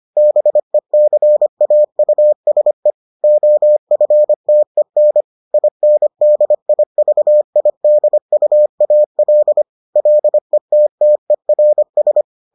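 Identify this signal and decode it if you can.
Morse code, heard as a keyed tone, about 25 wpm, 600 Hz. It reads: BECAUSE OFTEN INDIVIDUAL LETTERS
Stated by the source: Morse edition of the Foundations of Amateur Radio episodes.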